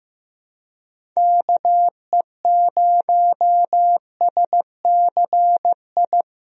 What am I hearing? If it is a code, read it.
Morse code, heard as a keyed tone, about 15 wpm, 700 Hz. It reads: KE0SCI
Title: USA Call Signs - 6 Characters - Repeated 2x - 15wpm